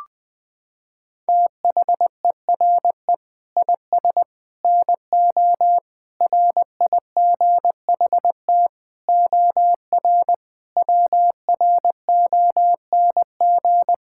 Morse code, heard as a keyed tone, about 21 wpm, 700 Hz. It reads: THERE IS NO RIGHT OR WRONG
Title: Sets of 8 Words - 20wpm